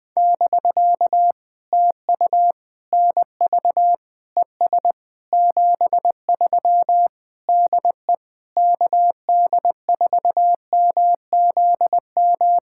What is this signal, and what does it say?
Morse code, heard as a keyed tone, about 20 wpm, 700 Hz. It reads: <BK> TU NV ES 73 DE KD4MZM
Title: POTA QSO Practice - 20wpm